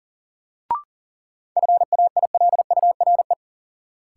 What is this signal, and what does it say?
Morse code, heard as a keyed tone, about 40 wpm, 700 Hz. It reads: FAILURE
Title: Sentences from Top 900 Words - New Word Drill - 40wpm